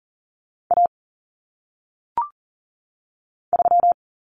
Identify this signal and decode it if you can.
Morse code, heard as a keyed tone, about 40 wpm, 700 Hz. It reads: A 3
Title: Common Cut Numbers - 1-Digit 40wpm